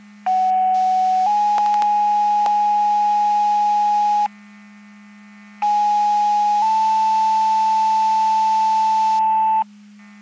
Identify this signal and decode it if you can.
Two-tone page, heard as a one-second tone, then a three-second tone, about 750 then 830 Hz, about 830 then 880 Hz. It is heard from 0.3 to 9.6 s.
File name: Portage / 00549 17-00 Portage 154.385 Paging